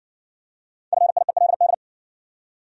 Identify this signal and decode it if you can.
Morse code, heard as a keyed tone, about 60 wpm, 700 Hz. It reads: FIELD